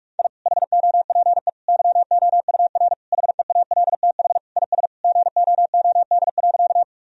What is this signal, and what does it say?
Morse code, heard as a keyed tone, about 45 wpm, 700 Hz. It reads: I HOPE YOUR HEALTH IS GOOD.
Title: Sentences from Top 1000 Words - Review Part 3 - 45wpm